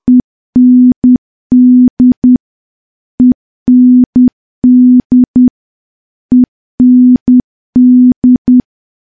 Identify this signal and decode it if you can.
Morse code, heard as a keyed tone, about 10 wpm, 250 Hz. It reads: END END END
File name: morse / morse-training-10WPM